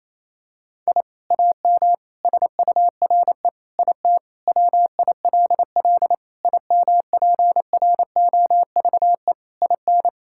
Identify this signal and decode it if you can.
Morse code, heard as a keyed tone, about 28 wpm, 700 Hz. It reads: I AM SURE IT WILL IMPROVE IN
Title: Sentences from Top 1000 Words - Review Part 5 - 28wpm